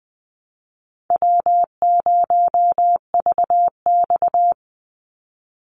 Morse code, heard as a keyed tone, about 20 wpm, 700 Hz. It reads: W0VX
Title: Call Signs Repeated 3x - 20wpm